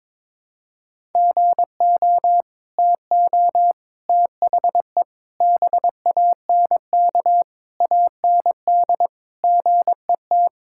Morse code, heard as a keyed tone, about 22 wpm, 700 Hz. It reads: GO TO THE BANK AND GET
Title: Sentences from Top 1000 Words - Review Part 5 - 22wpm